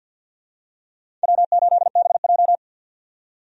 Morse code, heard as a keyed tone, about 50 wpm, 700 Hz. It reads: W8BJ
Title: USA Call Signs - 4 Characters - 50wpm